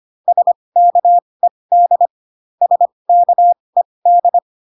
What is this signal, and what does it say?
Morse code, heard as a keyed tone, about 25 wpm, spaced as for 20 wpm, 700 Hz. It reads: SKED SKED